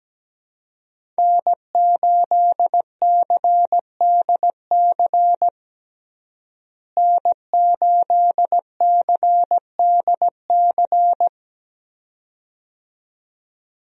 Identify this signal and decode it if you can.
Morse code, heard as a keyed tone, about 17 wpm, 700 Hz. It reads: N8CDC N8CDC
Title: USA Call Signs - 5 Characters - Repeated 2x - 17wpm